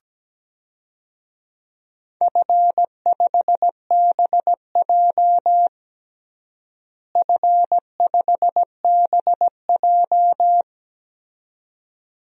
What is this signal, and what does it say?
Morse code, heard as a keyed tone, about 17 wpm, 700 Hz. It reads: F5BJ F5BJ
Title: DXCC-100 Call Signs - 4 Characters - Repeated 2x - 17wpm